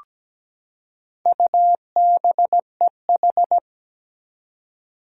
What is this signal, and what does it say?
Morse code, heard as a keyed tone, about 17 wpm, 700 Hz. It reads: UBEH
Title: Four Challenging Letter-Number Combinations 17wpm